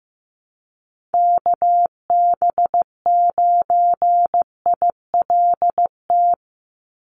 Morse code, heard as a keyed tone, about 15 wpm, 700 Hz. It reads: KB9ILT
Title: NA SOTA Chaser Callsigns 15wpm